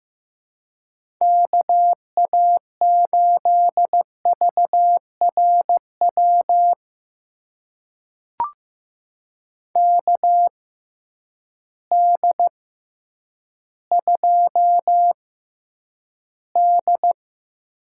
Morse code, heard as a keyed tone, about 16 wpm, spaced as for 9 wpm, 700 Hz. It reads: KA8VRW KD2D
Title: USA Call Signs - 6 Characters - 6x Character Spacing - 15wpm